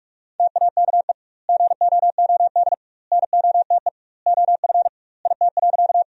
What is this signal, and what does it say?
Morse code, heard as a keyed tone, about 44 wpm, 700 Hz. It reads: TAKE GOOD NOTE OF IT.
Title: 2x Word-Spacing - Sentences from Top 500 Words - 45wpm